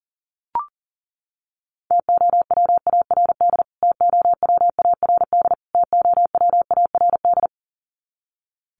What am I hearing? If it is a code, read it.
Morse code, heard as a keyed tone, about 40 wpm, 700 Hz. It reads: TOWARD TOWARD TOWARD
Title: Top 500 Words Repeated 40wpm